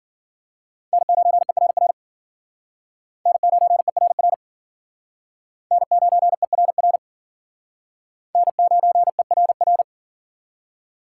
Morse code, heard as a keyed tone, about 52 wpm, 700 Hz. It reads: N9ERR N9ERR N9ERR N9ERR